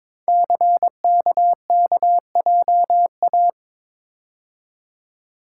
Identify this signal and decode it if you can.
Morse code, heard as a keyed tone, about 22 wpm, 700 Hz. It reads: CKKJA
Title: Code Groups - 5 Letters - 22wpm